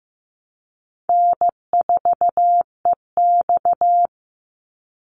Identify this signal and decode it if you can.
Morse code, heard as a keyed tone, about 15 wpm, 700 Hz. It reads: N4EX
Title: NA SOTA Chaser Callsigns 15wpm